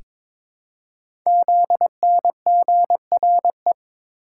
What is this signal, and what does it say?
Morse code, heard as a keyed tone, about 23 wpm, 700 Hz. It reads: ZNGRE
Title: Code Groups - 5 Letters - 22wpm